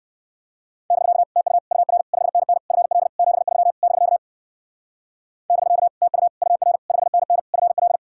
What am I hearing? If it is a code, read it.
Morse code, heard as a keyed tone, about 54 wpm, 700 Hz. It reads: <BK> TU UR 5NN FL 73 <BK> <BK> TU UR 5NN FL
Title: Speed-Racing Common QSO Phrases - 45wpm